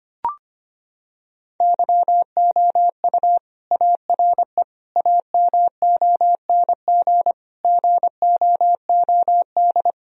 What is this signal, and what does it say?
Morse code, heard as a keyed tone, about 25 wpm, 700 Hz. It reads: YOU ARE AMONG GOOD
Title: Sentences from Top 1000 Words - Review Part 3 - 25wpm